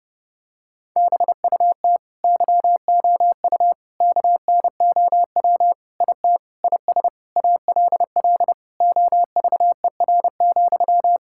Morse code, heard as a keyed tone, about 30 wpm, 700 Hz. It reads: BUT YOU KNOW IT IS ALL OVER,
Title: Sentences from Top 1600 Words 30wpm